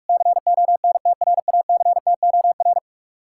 Morse code, heard as a keyed tone, about 59 wpm, spaced as for 52 wpm, 700 Hz. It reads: CONTRACTOR
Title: Speed-Racing Common QSO Phrases Encore - 45wpm